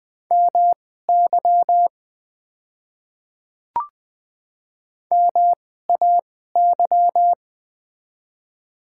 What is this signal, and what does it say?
Morse code, heard as a keyed tone, about 20 wpm, spaced as for 14 wpm, 700 Hz. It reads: MY MAY
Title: 2x Character Spacing - Top 100 Words - 20wpm